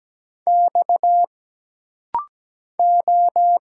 Morse code, heard as a keyed tone, about 17 wpm, 700 Hz. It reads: X O
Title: Single Letters - Rapid-Fire 17wpm